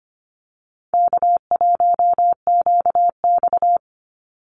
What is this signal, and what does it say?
Morse code, heard as a keyed tone, about 25 wpm, 700 Hz. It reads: K1QX
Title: Easy Call Signs - 25wpm